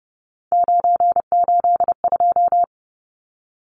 Morse code, heard as a keyed tone, about 30 wpm, 700 Hz. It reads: 982
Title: Numbers - 3-Digit 30wpm